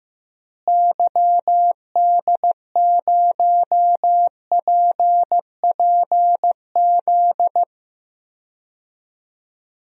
Morse code, heard as a keyed tone, about 15 wpm, 700 Hz. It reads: YD0PPZ
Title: DXCC-100 Call Signs - 6 Characters - Repeated 3x - 15wpm